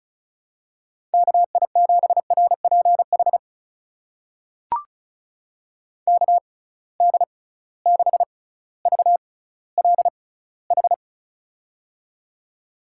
Morse code, heard as a keyed tone, about 36 wpm, spaced as for 18 wpm, 700 Hz. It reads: KI7RPH KD6VLH
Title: USA Call Signs - 6 Characters - 6x Character Spacing - 35wpm